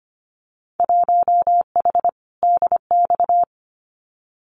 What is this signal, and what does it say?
Morse code, heard as a keyed tone, about 25 wpm, 700 Hz. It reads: 1H DX
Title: Winter Field Day Exchanges 25wpm